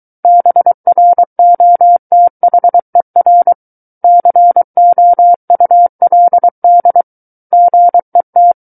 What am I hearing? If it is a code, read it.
Morse code, heard as a keyed tone, about 23 wpm, 700 Hz. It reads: BROTHER COULD GET
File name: Book One / 14-Chapter 14 - 23 WPM